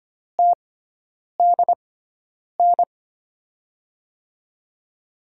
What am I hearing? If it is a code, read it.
Morse code, heard as a keyed tone, about 25 wpm, spaced as for 9 wpm, 700 Hz. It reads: TDN
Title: USA Call Signs - 5 Characters - 6x Character Spacing - 25wpm